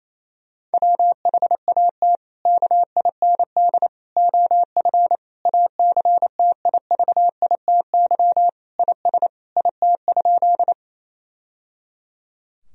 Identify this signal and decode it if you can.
Morse code, heard as a keyed tone, about 28 wpm, 700 Hz. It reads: WHAT KIND OF ACTIVITY IS IT?
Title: Sentences from Top 1000 Words - Review Part 4 - 28wpm